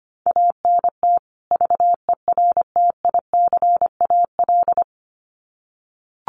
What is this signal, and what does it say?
Morse code, heard as a keyed tone, about 25 wpm, 700 Hz. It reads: ANT VERTICAL
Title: Common QSO Phrases 25wpm